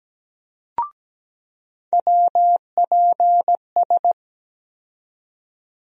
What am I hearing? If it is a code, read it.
Morse code, heard as a keyed tone, about 17 wpm, 700 Hz. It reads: WPS
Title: USA Call Sign Postfixes - 3 Letters 17wpm